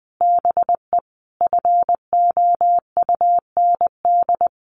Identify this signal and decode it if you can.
Morse code, heard as a keyed tone, about 20 wpm, 700 Hz. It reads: BE FOUND
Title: Sentences from Top 1500 Words 20wpm